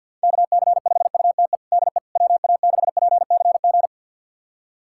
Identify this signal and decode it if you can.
Morse code, heard as a keyed tone, about 53 wpm, 700 Hz. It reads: KC5UTE DE WA6PCG
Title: USA Callsign DE Callsign 50wpm